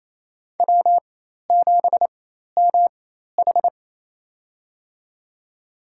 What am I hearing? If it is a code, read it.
Morse code, heard as a keyed tone, about 28 wpm, spaced as for 13 wpm, 700 Hz. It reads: W7MH
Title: USA Call Signs - 4 Characters - 4x Character Spacing - 28wpm